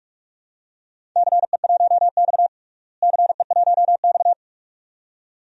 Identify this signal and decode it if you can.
Morse code, heard as a keyed tone, about 42 wpm, 700 Hz. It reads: CE1X CE1X
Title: DXCC-100 Call Signs - 4 Characters - Repeated 2x - 45wpm